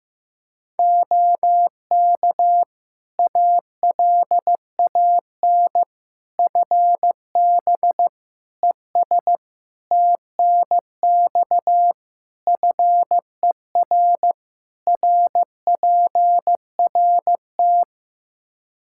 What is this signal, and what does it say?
Morse code, heard as a keyed tone, about 15 wpm, 700 Hz. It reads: OK ALAN FB ES TNX FER RPRT
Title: Practice QSO - Exchange 2 - Info - 15wpm